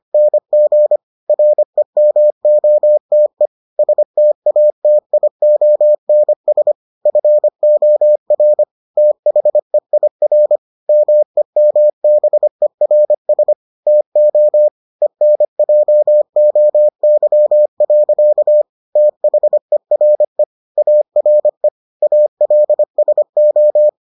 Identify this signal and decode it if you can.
Morse code, heard as a keyed tone, about 25 wpm, 600 Hz. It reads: NG REMOTE STATIONS FOR THEIR MEMBERS TO ENJOY. THERE ARE ALSO